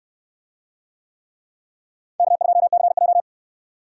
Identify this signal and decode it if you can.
Morse code, heard as a keyed tone, about 67 wpm, 700 Hz. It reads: K2CJ